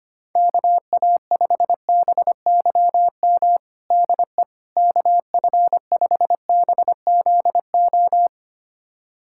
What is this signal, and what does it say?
Morse code, heard as a keyed tone, about 25 wpm, 700 Hz. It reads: KA5BYM DE KF5BZO
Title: USA Callsign DE Callsign 25wpm